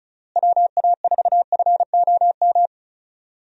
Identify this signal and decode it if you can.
Morse code, heard as a keyed tone, about 36 wpm, 700 Hz. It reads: WA4FOM
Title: USA Call Signs - 6 Characters - Repeated 3x - 35wpm